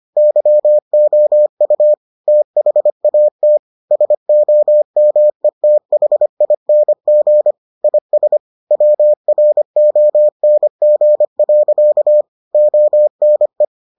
Morse code, heard as a keyed tone, about 25 wpm, 600 Hz. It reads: YOU THAT SOMETHING IS WRONG. ONE